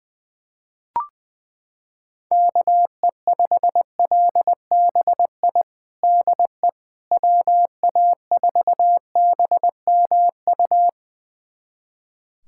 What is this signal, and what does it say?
Morse code, heard as a keyed tone, about 20 wpm, 700 Hz. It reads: KE5LBI DE WA4BMU
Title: USA Callsign DE Callsign - 6 Characters - 20wpm